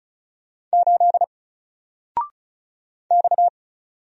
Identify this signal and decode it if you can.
Morse code, heard as a keyed tone, about 35 wpm, 700 Hz. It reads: 8 X
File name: Single-Letter-Number-35wpm